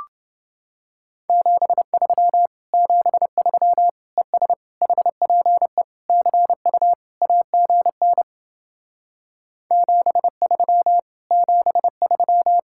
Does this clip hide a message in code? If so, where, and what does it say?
Yes: Morse code from 1.3 to 12.6 s, heard as a keyed tone, about 29 wpm, 700 Hz. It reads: 73 73 ES HPE CU AGN 73 73